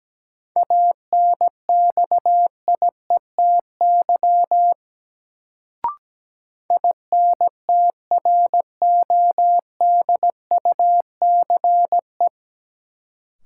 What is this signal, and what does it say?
Morse code, heard as a keyed tone, about 17 wpm, 700 Hz. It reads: ANXIETY INTRODUCE
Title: Sentences from Top 1600 Words - New Word Drill - 17wpm